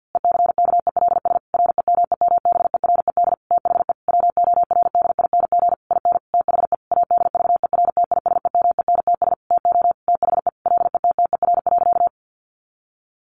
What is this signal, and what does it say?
Morse code, heard as a keyed tone, about 50 wpm, 700 Hz. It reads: ECKELS REMEMBERED THE WORDING IN THE ADVERTISEMENTS TO THE LETTER.
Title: A Sound Of Thunder 50wpm